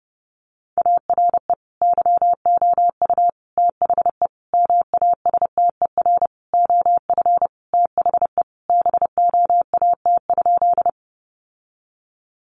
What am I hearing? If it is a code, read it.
Morse code, heard as a keyed tone, about 30 wpm, 700 Hz. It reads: ARE YOU THE MASTER OF THE BOAT?
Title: Sentences from Top 1300 Words 30wpm